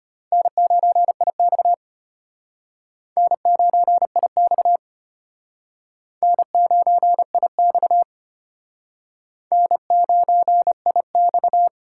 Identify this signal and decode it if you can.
Morse code, heard as a keyed tone, about 34 wpm, 700 Hz. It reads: N9IX N9IX N9IX N9IX